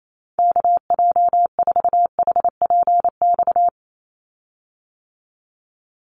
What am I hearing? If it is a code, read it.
Morse code, heard as a keyed tone, about 28 wpm, 700 Hz. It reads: KJ4HPX
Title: USA Call Signs 28wpm